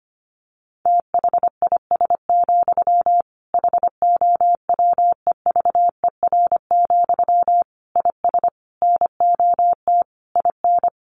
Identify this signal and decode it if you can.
Morse code, heard as a keyed tone, about 25 wpm, 700 Hz. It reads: THIS, HOWEVER, IS NOT IN
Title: Sentences from Top 1200 Words 25wpm